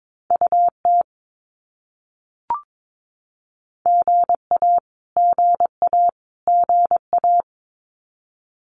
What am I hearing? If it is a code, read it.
Morse code, heard as a keyed tone, about 22 wpm, 700 Hz. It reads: UT GA GA GA
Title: US States Repeated 3x 22wpm